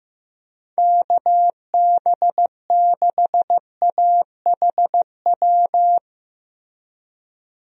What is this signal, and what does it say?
Morse code, heard as a keyed tone, about 16 wpm, 700 Hz. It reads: KB6AHW
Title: USA Call Signs - 6 Characters - Repeated 3x - 15wpm